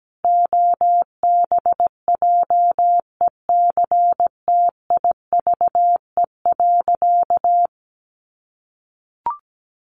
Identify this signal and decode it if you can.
Morse code, heard as a keyed tone, about 17 wpm, 700 Hz. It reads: OBJECTIVE.
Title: Sentences from Top 1200 Words 17wpm